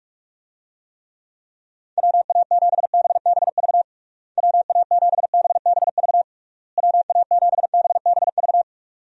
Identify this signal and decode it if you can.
Morse code, heard as a keyed tone, about 44 wpm, 700 Hz. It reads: WA7BBV WA7BBV WA7BBV
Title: USA Call Signs - 6 Characters - Repeated 3x - 45wpm